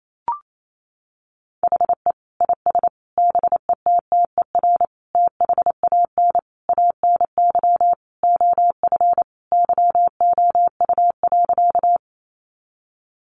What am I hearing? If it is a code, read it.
Morse code, heard as a keyed tone, about 28 wpm, 700 Hz. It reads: HE IS BETTER THAN ANY OF YOU.
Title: Sentences from Top 400 Words 28wpm